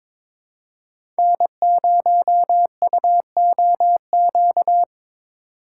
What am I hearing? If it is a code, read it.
Morse code, heard as a keyed tone, about 23 wpm, 700 Hz. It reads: N0UOQ
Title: USA Call Signs - 5 Characters - 3x Character Spacing - 22wpm